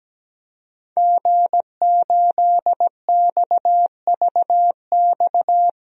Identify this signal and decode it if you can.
Morse code, heard as a keyed tone, about 17 wpm, 700 Hz. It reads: G8XVX
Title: DXCC-100 Call Signs - 5 Characters - Repeated 2x - 17wpm